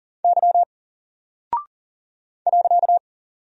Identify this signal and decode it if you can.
Morse code, heard as a keyed tone, about 38 wpm, 700 Hz. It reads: Y .
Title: Single Character (Letter:Number:Punctuation:Prosign) Alt 40wpm